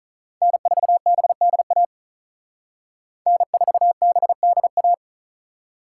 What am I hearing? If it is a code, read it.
Morse code, heard as a keyed tone, about 46 wpm, 700 Hz. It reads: N4BDA N4BDA